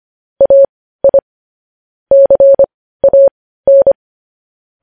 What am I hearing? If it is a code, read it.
Morse code, heard as a keyed tone, about 25 wpm, spaced as for 15 wpm, 550 Hz. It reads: AI CAN
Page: news-headlines-in-morse-code-at-15-wpm